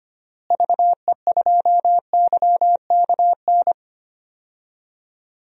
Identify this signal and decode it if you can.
Morse code, heard as a keyed tone, about 25 wpm, 700 Hz. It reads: VE2YKN
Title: DXCC-100 Call Signs - 6 Characters - Repeated 3x - 25wpm